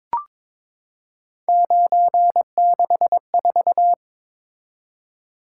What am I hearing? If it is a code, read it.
Morse code, heard as a keyed tone, about 23 wpm, 700 Hz. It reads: 964
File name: practice / 22wpm / Three Letter-Number Combinations - 22wpm